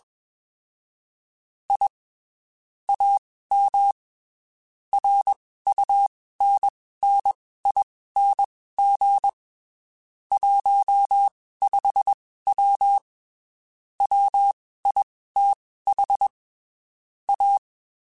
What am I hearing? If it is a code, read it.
Morse code, heard as a keyed tone, about 21 wpm, spaced as for 15 wpm, 800 Hz. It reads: I AM RUNNING 15W WITH A
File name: (spaced) / QSOs Spaced-mp3s / WD0BGP 1 @20 sp